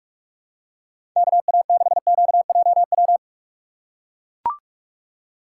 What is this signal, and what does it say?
Morse code, heard as a keyed tone, about 44 wpm, 700 Hz. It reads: KA6QJW
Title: USA Call Signs - 6 Characters - 2x Character Spacing - 45wpm